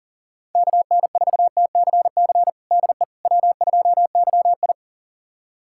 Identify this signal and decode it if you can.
Morse code, heard as a keyed tone, about 37 wpm, 700 Hz. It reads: KN4TCC DE W2YI
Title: USA Callsign DE Callsign 40wpm